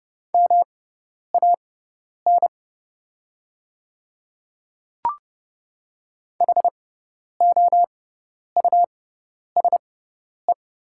Morse code, heard as a keyed tone, about 30 wpm, spaced as for 10 wpm, 690 Hz. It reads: MAN HOUSE
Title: 6x Character Spacing - Top 200 Words - 30wpm